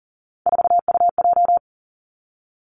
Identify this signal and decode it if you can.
Morse code, heard as a keyed tone, about 42 wpm, 700 Hz. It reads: 4UJ
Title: Three Challenging Letter-Number Combinations 40wpm